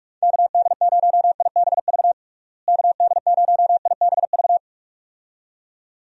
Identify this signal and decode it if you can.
Morse code, heard as a keyed tone, about 46 wpm, 700 Hz. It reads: KD0IBV KD0IBV
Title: USA Call Signs - 6 Characters - Repeated 2x - 45wpm